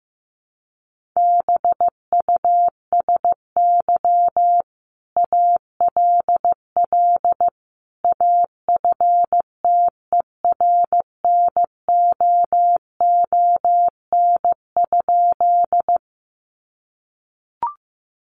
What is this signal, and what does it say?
Morse code, heard as a keyed tone, about 15 wpm, 700 Hz. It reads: BUSY ALL AFTERNOON?
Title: Sentences from Top 1500 Words 15wpm